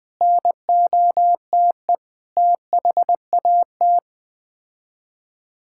Morse code, heard as a keyed tone, about 20 wpm, 700 Hz. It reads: NOTE THAT
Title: Sets of 6 Words - 20wpm